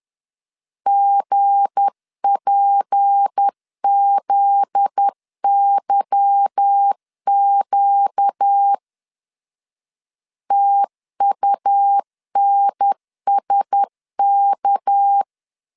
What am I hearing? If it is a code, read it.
Morse code, heard as a keyed tone, about 10 wpm, 800 Hz. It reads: GPZYQ TUNSK
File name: LEZIONI / 8 YJ45 / 8 Y 10wpm 2615